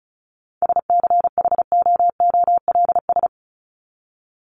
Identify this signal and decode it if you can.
Morse code, heard as a keyed tone, about 34 wpm, 700 Hz. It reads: SCHOOLS
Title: ES:S Suffix - Top 100 Words with Base Words 35wpm